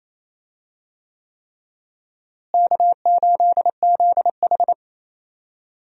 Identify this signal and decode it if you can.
Morse code, heard as a keyed tone, about 28 wpm, 700 Hz. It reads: K8ZH
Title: USA Call Signs - 4 Characters - Repeated 3x - 28wpm